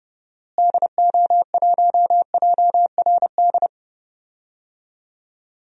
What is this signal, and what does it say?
Morse code, heard as a keyed tone, about 30 wpm, 700 Hz. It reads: DO1JRD
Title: DXCC-100 Call Signs - 6 Characters - Repeated 3x - 30wpm